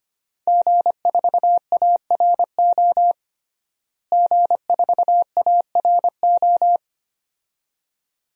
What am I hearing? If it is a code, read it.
Morse code, heard as a keyed tone, about 25 wpm, 700 Hz. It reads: G4ARO G4ARO
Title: DXCC-100 Call Signs - 5 Characters - Repeated 2x - 25wpm